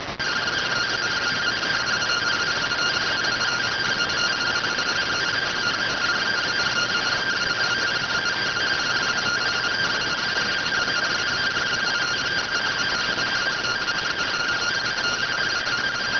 Two-tone teleprinter signal, about 170 Hz shift.